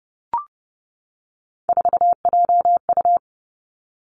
Morse code, heard as a keyed tone, about 30 wpm, 700 Hz. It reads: 4JU
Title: Three Challenging Letter-Number Combinations 30wpm